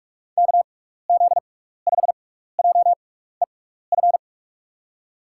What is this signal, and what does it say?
Morse code, heard as a keyed tone, about 42 wpm, spaced as for 15 wpm, 700 Hz. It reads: KZ5JEF
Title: USA Call Signs - 6 Characters - 6x Character Spacing - 45wpm